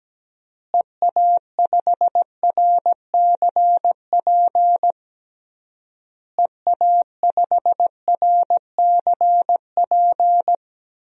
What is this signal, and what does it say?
Morse code, heard as a keyed tone, about 17 wpm, 700 Hz. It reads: EA5RCP EA5RCP
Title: DXCC-100 Call Signs - 6 Characters - Repeated 3x - 17wpm